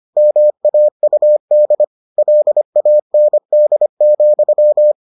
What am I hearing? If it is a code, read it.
Morse code, heard as a keyed tone, about 25 wpm, 600 Hz. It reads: MAUD LAND,